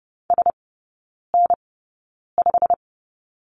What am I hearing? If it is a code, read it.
Morse code, heard as a keyed tone, about 30 wpm, spaced as for 9 wpm, 700 Hz. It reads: SN5
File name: course / 30:10wpm / 087 Lesson 15e - Number 5 - Three Characters 30wpm